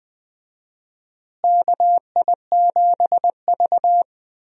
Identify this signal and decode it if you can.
Morse code, heard as a keyed tone, about 20 wpm, 700 Hz. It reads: KI7V
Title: USA Call Signs - 4 Characters - Repeated 2x - 20wpm